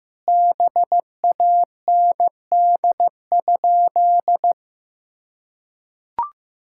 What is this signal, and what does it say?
Morse code, heard as a keyed tone, about 16 wpm, 700 Hz. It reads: BAND?
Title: Sentences from Top 1600 Words 15wpm